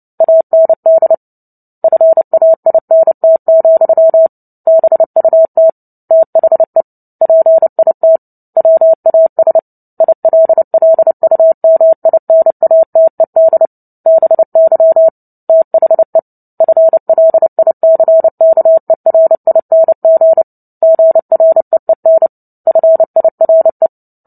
Morse code, heard as a keyed tone, about 30 wpm, 650 Hz. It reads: AND FAINT, BUT THE PIT WAS ILLUMINATED BY THE FLICKERING GREEN FIRE
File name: Book Two / 03-Chapter 3 - 29 WPM